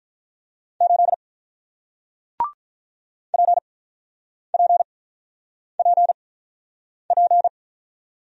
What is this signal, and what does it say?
Morse code, heard as a keyed tone, about 48 wpm, spaced as for 9 wpm, 700 Hz. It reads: 8 PPPP